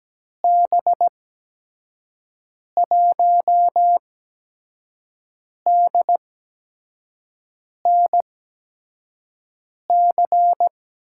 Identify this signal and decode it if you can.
Morse code, heard as a keyed tone, about 17 wpm, spaced as for 5 wpm, 700 Hz. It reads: B1DNC
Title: USA Call Signs - 6 Characters - 8x Character Spacing - 17wpm